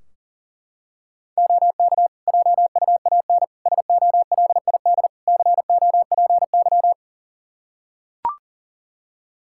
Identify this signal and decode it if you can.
Morse code, heard as a keyed tone, about 40 wpm, 700 Hz. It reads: OK JUAN SOLID COPY
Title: Practice QSO - Exchange 3 - More Info - 40wpm